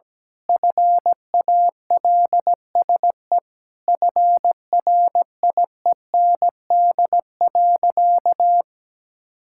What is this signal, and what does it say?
Morse code, heard as a keyed tone, about 17 wpm, 700 Hz. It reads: FALSE FRIEND.
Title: Sentences from Top 1600 Words 17wpm